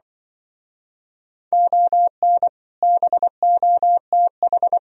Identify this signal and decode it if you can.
Morse code, heard as a keyed tone, about 25 wpm, 700 Hz. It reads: ON BOTH